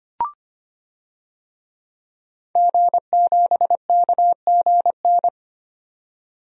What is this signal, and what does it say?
Morse code, heard as a keyed tone, about 25 wpm, 700 Hz. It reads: G7KGN
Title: DXCC-100 Call Signs - 5 Characters - 25wpm